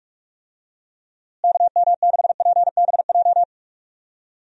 Morse code, heard as a keyed tone, about 45 wpm, 700 Hz. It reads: KM6PBJ
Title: USA Call Signs - 6 Characters - Repeated 2x - 45wpm